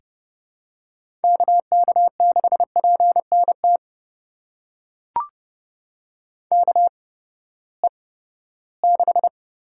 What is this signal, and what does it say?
Morse code, heard as a keyed tone, about 30 wpm, spaced as for 17 wpm, 700 Hz. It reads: KK6PNT KE6
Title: USA Call Signs - 6 Characters - 8x Character Spacing - 30wpm